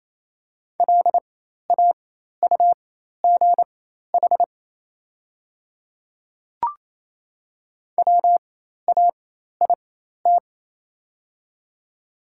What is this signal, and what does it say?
Morse code, heard as a keyed tone, about 28 wpm, spaced as for 13 wpm, 700 Hz. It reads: LAUGH WAIT
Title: 4x Character Spacing - Top 500 Words - 28wpm